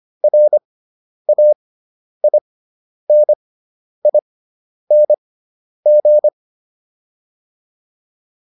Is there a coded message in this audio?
Yes: Morse code from 0.2 to 6.3 s, heard as a keyed tone, about 25 wpm, spaced as for 10 wpm, 600 Hz. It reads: RAINING